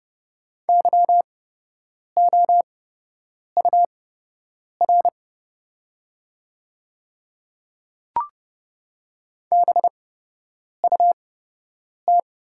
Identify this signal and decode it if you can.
Morse code, heard as a keyed tone, about 30 wpm, spaced as for 8 wpm, 700 Hz. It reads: YOUR BUT
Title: 8x Character Spacing - Top 100 Words - 30wpm